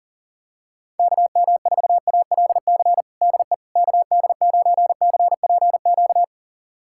Morse code, heard as a keyed tone, about 40 wpm, 700 Hz. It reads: KM4ALC DE KD9CPQ
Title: USA Callsign DE Callsign 40wpm